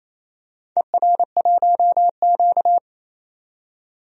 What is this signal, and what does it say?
Morse code, heard as a keyed tone, about 29 wpm, 700 Hz. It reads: ER1Q
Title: DXCC-100 Call Signs - 4 Characters - 28wpm